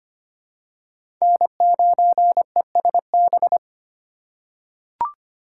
Morse code, heard as a keyed tone, about 25 wpm, 700 Hz. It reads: N9ESB
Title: USA Call Signs - 5 Characters - 4x Character Spacing - 25wpm